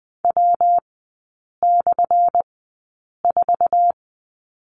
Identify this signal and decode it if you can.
Morse code, heard as a keyed tone, about 20 wpm, spaced as for 8 wpm, 700 Hz. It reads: W/4
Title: 178 Lesson 29e - Slash - Three Characters 20wpm